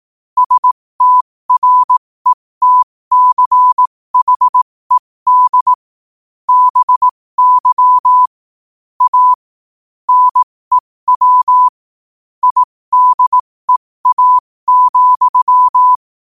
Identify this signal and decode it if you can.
Morse code, heard as a keyed tone, about 18 wpm, spaced as for 15 wpm, 1000 Hz. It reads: STRETCHED BY A NEW IDEA,
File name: qotdfiles / 15wpm